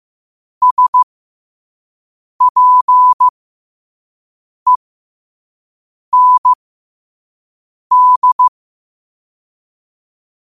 Morse code, heard as a keyed tone, about 15 wpm, spaced as for 5 wpm, 1000 Hz. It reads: SPEND